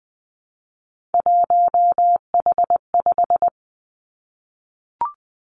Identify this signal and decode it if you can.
Morse code, heard as a keyed tone, about 20 wpm, 700 Hz. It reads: 1H5
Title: Three Challenging Letter-Number Combinations 20wpm